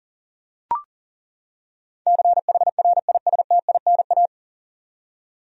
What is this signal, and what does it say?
Morse code, heard as a keyed tone, about 37 wpm, 700 Hz. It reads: CHRISTINA
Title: USA Common Names - 40wpm